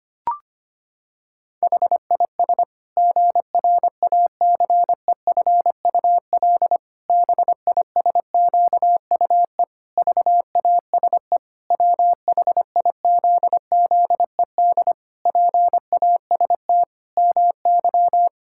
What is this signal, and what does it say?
Morse code, heard as a keyed tone, about 25 wpm, 700 Hz. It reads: HIS GRACEFUL BISQUE VASE WHIZZED PAST MY
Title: Pangram Challenge v2 - 25wpm